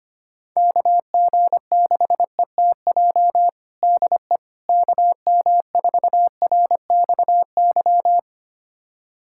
Morse code, heard as a keyed tone, about 25 wpm, 700 Hz. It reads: KG6ETJ DE KM4RXY
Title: USA Callsign DE Callsign 25wpm